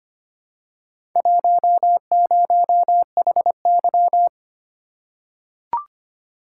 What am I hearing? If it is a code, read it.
Morse code, heard as a keyed tone, about 25 wpm, 700 Hz. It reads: 10HY
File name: practice / 25wpm / Four Letter-Number Combinations - 25wpm